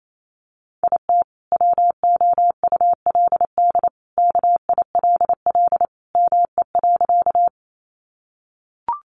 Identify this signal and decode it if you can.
Morse code, heard as a keyed tone, about 28 wpm, 700 Hz. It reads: IT WOULD KILL ME.
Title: Sentences from Top 1400 Words 28wpm